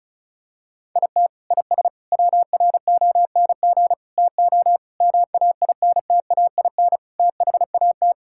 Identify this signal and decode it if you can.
Morse code, heard as a keyed tone, about 34 wpm, 700 Hz. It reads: IT IS WRONG TO MAINTAIN THAT
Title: Sentences from Top 1000 Words - Review Part 5 - 35wpm